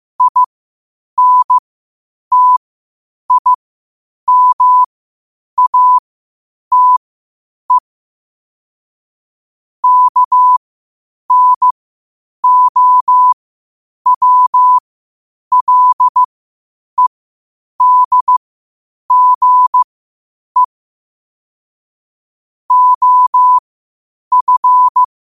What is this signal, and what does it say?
Morse code, heard as a keyed tone, about 15 wpm, spaced as for 8 wpm, 1000 Hz. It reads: INTIMATE KNOWLEDGE OF
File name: qotdfiles / 8wpm